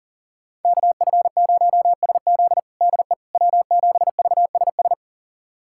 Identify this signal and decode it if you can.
Morse code, heard as a keyed tone, about 37 wpm, 700 Hz. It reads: KF0SZ DE W7VSS
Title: USA Callsign DE Callsign - 5 Characters - 40wpm